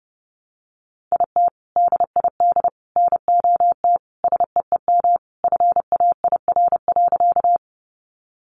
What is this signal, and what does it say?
Morse code, heard as a keyed tone, about 30 wpm, 700 Hz. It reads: IT DID NOT SEEM FAIR.
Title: Sentences from Top 1300 Words 30wpm